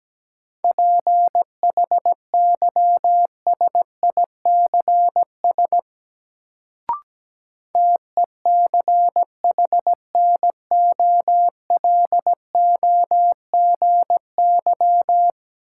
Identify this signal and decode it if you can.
Morse code, heard as a keyed tone, about 17 wpm, 700 Hz. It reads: PHYSICS TECHNOLOGY
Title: Sentences from Top 1000 Words - Review Part 5 - New Word Drill - 17wpm